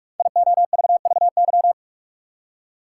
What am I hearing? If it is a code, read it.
Morse code, heard as a keyed tone, about 44 wpm, 700 Hz. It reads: IOVVY
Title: Code Groups - 5 Letter:Numbers - 45wpm